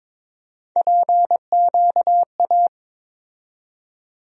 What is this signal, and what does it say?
Morse code, heard as a keyed tone, about 22 wpm, 700 Hz. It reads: PQA
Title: USA Call Sign Postfixes - 3 Letters 22wpm